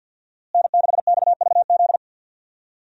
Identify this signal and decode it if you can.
Morse code, heard as a keyed tone, about 46 wpm, 700 Hz. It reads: N6XVZ
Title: USA Call Signs - 5 Characters - Repeated 3x - 50wpm